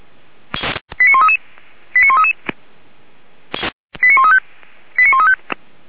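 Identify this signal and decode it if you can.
Five-tone selective call (ZVEI), heard as quick runs of five tones, tones of about 70 ms. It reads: 89133 89133 89135 89135
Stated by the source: ZVEI five-tone selective call received on an SDR, from the Feuerwehr Orlamünde fire brigade.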